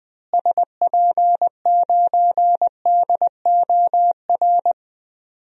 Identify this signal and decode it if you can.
Morse code, heard as a keyed tone, about 21 wpm, 700 Hz. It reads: SP9DOR
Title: DXCC-100 Call Signs - 6 Characters - Repeated 3x - 20wpm